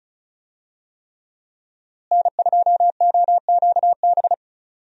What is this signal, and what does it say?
Morse code, heard as a keyed tone, about 35 wpm, 700 Hz. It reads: N2OQB
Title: USA Call Signs - 5 Characters - Repeated 3x - 35wpm